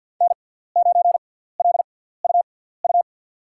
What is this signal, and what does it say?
Morse code, heard as a keyed tone, about 51 wpm, spaced as for 17 wpm, 700 Hz. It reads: N9LUU
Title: USA Call Signs - 5 Characters - 6x Character Spacing - 50wpm